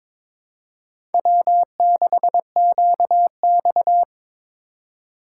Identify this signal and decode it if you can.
Morse code, heard as a keyed tone, about 22 wpm, 700 Hz. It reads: W6QX
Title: USA Call Signs - 4 Characters - 2x Character Spacing - 22wpm